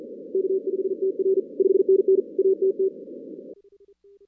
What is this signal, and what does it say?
Morse code, heard as a keyed tone, about 45 wpm, 380 Hz. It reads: K5TR 5NN ATT
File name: 2016 arrl dx cw / 2016-ARRL-DX-CW-K5TR- - 0-17